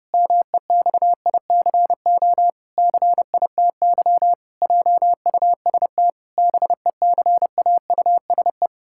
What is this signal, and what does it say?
Morse code, heard as a keyed tone, about 30 wpm, 700 Hz. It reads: MEXICO CITY JUST BECAUSE
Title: Pangram Challenge v2 - 30wpm